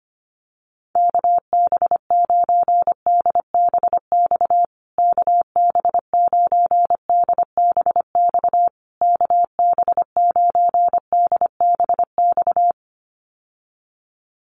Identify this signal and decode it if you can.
Morse code, heard as a keyed tone, about 25 wpm, 700 Hz. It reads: KB9DBX KB9DBX KB9DBX
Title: USA Call Signs Repeated 3x 25wpm